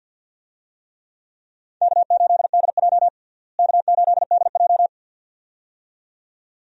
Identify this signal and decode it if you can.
Morse code, heard as a keyed tone, about 52 wpm, 700 Hz. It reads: K8DJ K8DJ
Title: USA Call Signs - 4 Characters - Repeated 2x - 50wpm